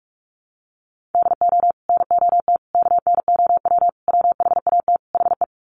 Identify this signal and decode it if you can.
Morse code, heard as a keyed tone, about 44 wpm, 690 Hz. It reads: DO NOT KNOW WHAT HE
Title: Sets of 5 Words Encore2 45wpm